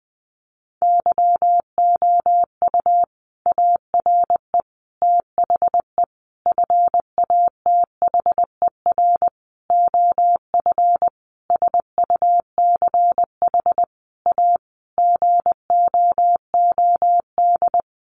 Morse code, heard as a keyed tone, about 19 wpm, 700 Hz. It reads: YOU ARE THE FATHER OF SUCH A GOOD
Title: Sentences from Top 1500 Words 20wpm